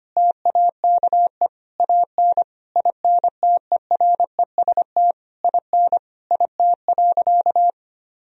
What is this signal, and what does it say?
Morse code, heard as a keyed tone, about 25 wpm, 700 Hz. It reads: TAKE AN INTEREST IN IT.
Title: Sentences from Top 1000 Words - Review Part 2 - 25wpm